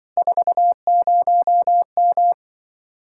Morse code, heard as a keyed tone, about 25 wpm, 700 Hz. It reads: 40M
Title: Speed-Racing Common QSO Phrases Encore - 20wpm